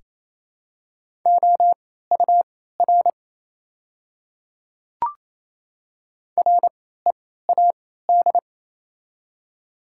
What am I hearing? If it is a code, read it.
Morse code, heard as a keyed tone, about 28 wpm, spaced as for 16 wpm, 700 Hz. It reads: OUR READ
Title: 3x Character Spacing - Top 200 Words - 28wpm